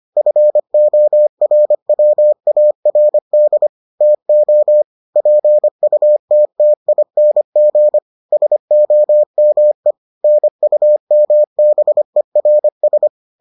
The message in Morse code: FORWARD TO PUTTING SOME NUMBERS

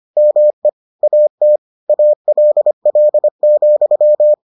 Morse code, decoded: ME AT ALL,

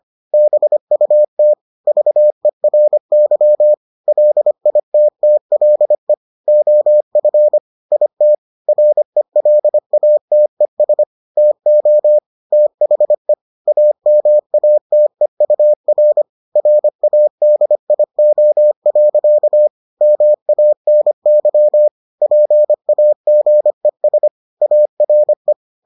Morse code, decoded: BUT VERY LITTLE OF IT RELATES TO THE AMATEUR RADIO. MANY PAGES ARE

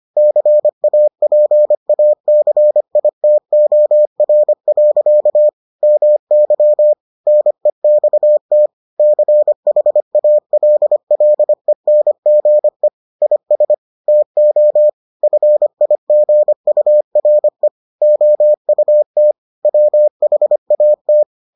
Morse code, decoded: CAPACITOR. MY NEXT CHALLENGE IS TO FIGURE OUT WHAT

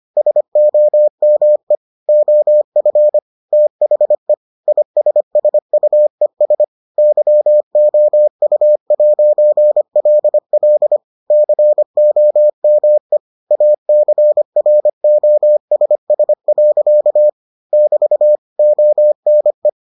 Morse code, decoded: SOME OF THE ISSUES YOU'LL COME ACROSS. <BT> ONE

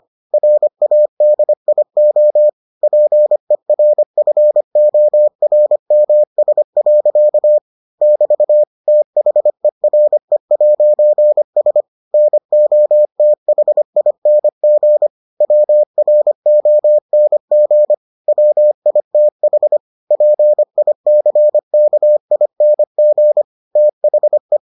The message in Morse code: RADIO PERFORMS. <BT> THERE'S NOTHING WRONG WITH PICKING THE

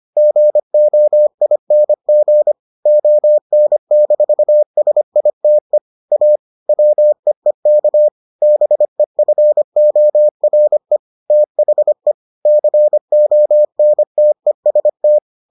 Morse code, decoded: GOING ON-SITE A WEEK BEFORE THE CONTEST

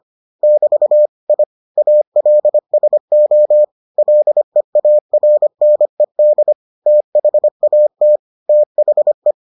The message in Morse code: <BT> I ALSO LEARNED THAT THE

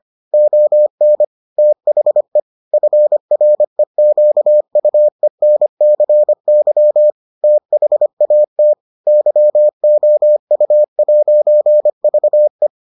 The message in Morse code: ON THE FREQUENCY THAT YOU'VE

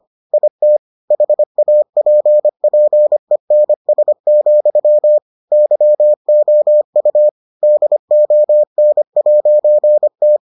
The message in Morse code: IT HAPPENS, YOU DON'T

